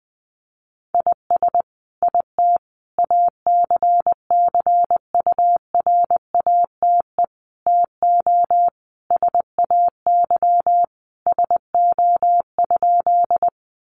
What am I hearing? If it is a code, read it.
Morse code, heard as a keyed tone, about 20 wpm, 700 Hz. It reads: IS IT ACCURATE TO SAY SO?
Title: Sentences from Top 1200 Words 20wpm